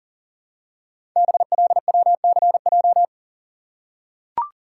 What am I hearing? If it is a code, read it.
Morse code, heard as a keyed tone, about 42 wpm, 690 Hz. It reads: BLWCJ
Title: Code Groups - 5 Letters - 40wpm